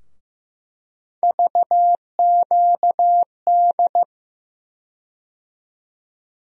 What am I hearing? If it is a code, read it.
Morse code, heard as a keyed tone, about 15 wpm, 700 Hz. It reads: VQD
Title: USA Call Sign Postfixes - 3 Letters 15wpm